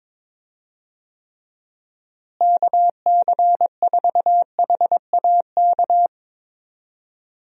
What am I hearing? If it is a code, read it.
Morse code, heard as a keyed tone, about 22 wpm, 700 Hz. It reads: KC4HAK
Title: USA Call Signs - 6 Characters - Repeated 2x - 22wpm